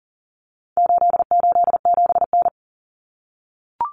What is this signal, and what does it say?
Morse code, heard as a keyed tone, about 37 wpm, 700 Hz. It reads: 887N
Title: Common Cut Numbers - 4-Digit 40wpm